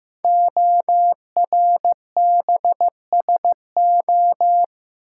Morse code, heard as a keyed tone, about 15 wpm, 700 Hz. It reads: ORBSO
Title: Code Groups - 5 Letter:Numbers - 15wpm